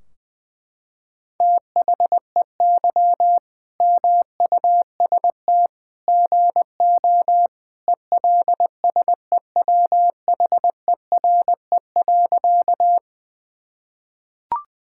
Morse code, heard as a keyed tone, about 21 wpm, 700 Hz. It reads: THEY MUST GO ELSEWHERE.
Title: Sentences from Top 1600 Words 20wpm